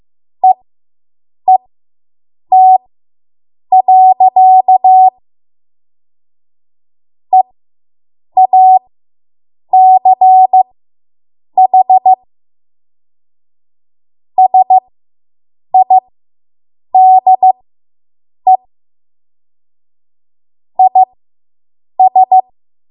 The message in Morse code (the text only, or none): EET. EACH SIDE IS